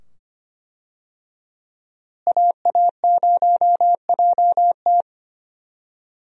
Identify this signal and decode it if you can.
Morse code, heard as a keyed tone, about 25 wpm, 700 Hz. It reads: AA0JT
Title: USA Call Signs - 5 Characters - Repeated 2x - 25wpm